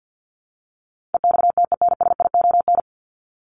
Morse code, heard as a keyed tone, about 50 wpm, 700 Hz. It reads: EXTENSION
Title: Top 1501-1750 Words 50wpm